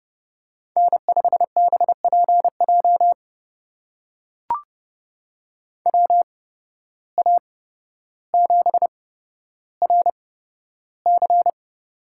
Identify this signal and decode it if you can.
Morse code, heard as a keyed tone, about 30 wpm, spaced as for 13 wpm, 700 Hz. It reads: N5BPJ WA7RC
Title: USA Call Signs - 5 Characters - 8x Character Spacing - 30wpm